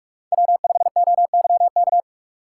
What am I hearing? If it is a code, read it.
Morse code, heard as a keyed tone, about 47 wpm, 700 Hz. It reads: W5OYK
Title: USA Call Signs - 5 Characters - 6x Character Spacing - 45wpm